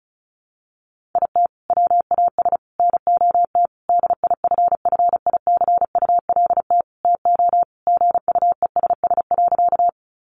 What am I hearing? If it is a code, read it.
Morse code, heard as a keyed tone, about 34 wpm, 700 Hz. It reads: IT WAS NOT DIFFICULT TO GUESS.
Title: Sentences from Top 1500 Words 35wpm